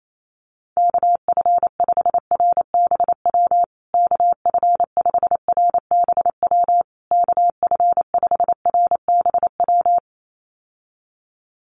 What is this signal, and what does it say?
Morse code, heard as a keyed tone, about 28 wpm, 700 Hz. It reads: KF5RBW KF5RBW KF5RBW
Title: USA Call Signs Repeated 3x 28wpm